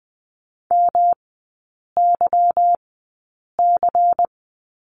Morse code, heard as a keyed tone, about 20 wpm, spaced as for 8 wpm, 700 Hz. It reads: MYC
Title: 145 Lesson 24e - Letter y - Three Characters 20wpm